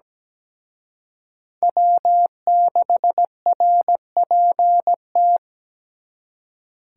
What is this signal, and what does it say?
Morse code, heard as a keyed tone, about 20 wpm, 700 Hz. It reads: W6RPT
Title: Speed Racing - USA 1x3 Call Signs 17wpm